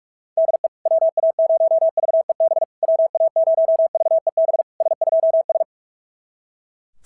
Morse code, heard as a keyed tone, about 46 wpm, 660 Hz. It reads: DE WA0VEB WA0VEB S2S
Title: SOTA QSO Practice 45wpm